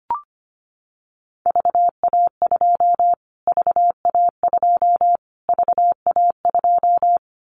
Morse code, heard as a keyed tone, about 25 wpm, 700 Hz. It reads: VA2 VA2 VA2
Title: Callsign Prefixes for Top 25 DXCC - Repeated 3x - 25wpm